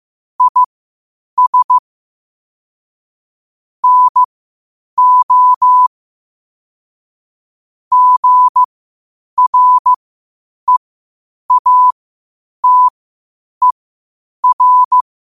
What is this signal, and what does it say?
Morse code, heard as a keyed tone, about 15 wpm, spaced as for 8 wpm, 1000 Hz. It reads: IS NO GREATER
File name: qotdfiles / 8wpm